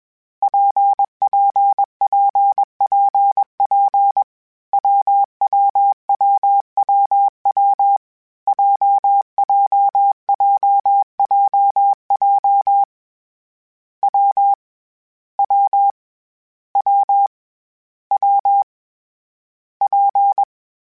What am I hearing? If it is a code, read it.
Morse code, heard as a keyed tone, about 21 wpm, 800 Hz. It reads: PPPPP WWWWW JJJJJ W W W W P
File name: Difficult Combos / Difficult Combos-mp3s / WPJ WPJ1 @20